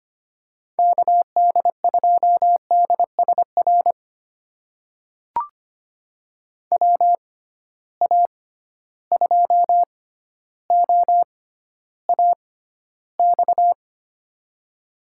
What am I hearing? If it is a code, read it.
Morse code, heard as a keyed tone, about 25 wpm, spaced as for 13 wpm, 700 Hz. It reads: KD2DSR WA2OAX
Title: USA Call Signs - 6 Characters - 6x Character Spacing - 25wpm